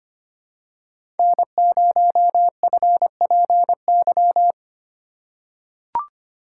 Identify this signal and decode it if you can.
Morse code, heard as a keyed tone, about 25 wpm, 700 Hz. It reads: N0FPY
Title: USA Call Signs - 5 Characters - 2x Character Spacing - 25wpm